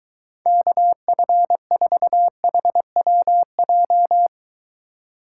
Morse code, heard as a keyed tone, about 23 wpm, 700 Hz. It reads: KF4HWJ